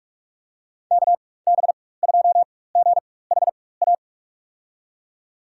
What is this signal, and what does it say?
Morse code, heard as a keyed tone, about 46 wpm, spaced as for 21 wpm, 700 Hz. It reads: KB2GHA
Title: USA Call Signs - 6 Characters - 4x Character Spacing - 45wpm